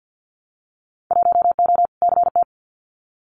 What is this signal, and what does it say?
Morse code, heard as a keyed tone, about 48 wpm, 700 Hz. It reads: 1O CT
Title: Winter Field Day Exchanges 50wpm